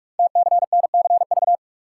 Morse code, heard as a keyed tone, about 44 wpm, 700 Hz. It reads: TCNCV
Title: Code Groups - 5 Letters - 45wpm